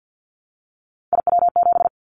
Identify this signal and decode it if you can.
Morse code, heard as a keyed tone, about 50 wpm, 700 Hz. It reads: IW7
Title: Callsign Prefixes for Top 25 DXCC - Repeated 3x - 50wpm